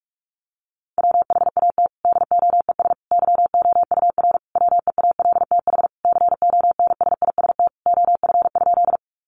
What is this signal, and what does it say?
Morse code, heard as a keyed tone, about 44 wpm, 700 Hz. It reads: WHAT DOES YOUR WEALTH CONSIST OF?
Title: Sentences from Top 1400 Words 45wpm